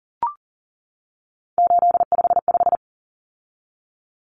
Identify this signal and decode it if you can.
Morse code, heard as a keyed tone, about 40 wpm, 700 Hz. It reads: 855
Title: Common Cut Numbers - 3-Digit 40wpm